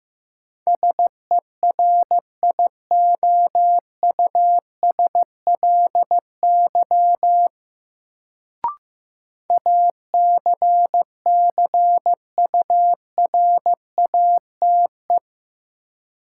Morse code, heard as a keyed tone, about 15 wpm, 700 Hz. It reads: SERIOUSLY ACCURATE
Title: Sentences from Top 1200 Words - New Word Drill - 15wpm